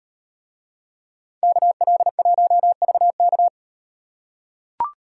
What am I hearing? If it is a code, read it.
Morse code, heard as a keyed tone, about 38 wpm, 700 Hz. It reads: KL1VK